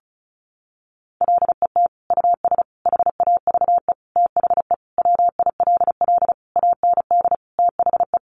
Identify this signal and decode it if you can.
Morse code, heard as a keyed tone, about 34 wpm, 700 Hz. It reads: LET US HAVE THE WILL AND THE
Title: Sentences from Top 1300 Words 35wpm